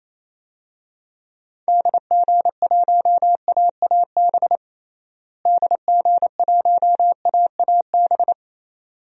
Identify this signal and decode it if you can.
Morse code, heard as a keyed tone, about 28 wpm, 700 Hz. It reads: DG1AAB DG1AAB
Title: DXCC-100 Call Signs - 6 Characters - Repeated 2x - 28wpm